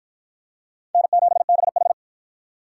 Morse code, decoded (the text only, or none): N7BH